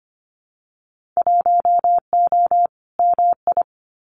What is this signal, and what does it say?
Morse code, heard as a keyed tone, about 25 wpm, 700 Hz. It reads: 1O MI
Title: Winter Field Day Exchanges Encore 25wpm